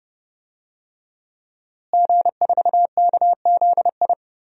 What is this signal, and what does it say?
Morse code, heard as a keyed tone, about 32 wpm, 700 Hz. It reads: G4KZI
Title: DXCC-100 Call Signs - 5 Characters - Repeated 2x - 30wpm